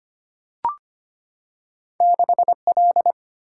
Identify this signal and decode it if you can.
Morse code, heard as a keyed tone, about 25 wpm, 700 Hz. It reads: 6L
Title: Single-Letter Single-Number 25wpm